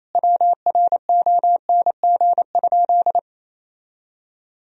Morse code, heard as a keyed tone, about 28 wpm, 700 Hz. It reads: WRONG?
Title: Sentences from Top 1000 Words - Review Part 1 - 28wpm